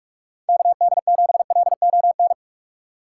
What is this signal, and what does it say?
Morse code, heard as a keyed tone, about 47 wpm, 700 Hz. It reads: KD7LON
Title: USA Call Signs - 4x Character Spacing - 45wpm